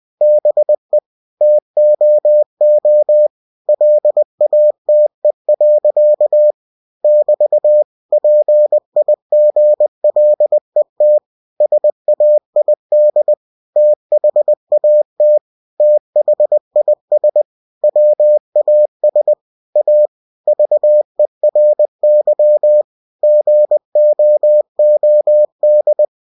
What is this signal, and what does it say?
Morse code, heard as a keyed tone, about 20 wpm, 600 Hz. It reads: BE TOO LATE. <BT> PIGLET SAID THAT THIS WAS A VERY GOOD
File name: cw-books / winnie-the-pooh / winnie-20wpm-page-0035